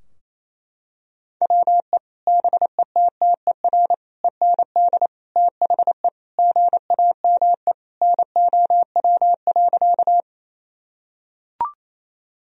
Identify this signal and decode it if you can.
Morse code, heard as a keyed tone, about 28 wpm, 700 Hz. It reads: WE BETTER END THE GAME NOW.
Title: Sentences from Top 1000 Words - Review Part 3 - 28wpm